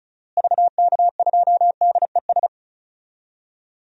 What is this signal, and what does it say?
Morse code, heard as a keyed tone, about 34 wpm, 700 Hz. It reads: VK2DES
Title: DXCC-100 Call Signs - 6 Characters - Repeated 2x - 35wpm